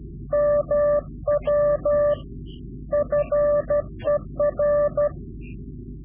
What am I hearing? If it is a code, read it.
Morse code, heard as a keyed tone, about 12 wpm, 590 Hz. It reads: MW FER
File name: CW-QSO / Directional Call / Directional QSO